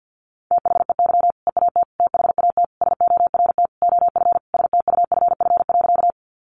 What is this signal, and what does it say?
Morse code, heard as a keyed tone, about 50 wpm, 700 Hz. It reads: THEY EAT THAT SORT OF STUFF.
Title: Sentences from Top 1400 Words 50wpm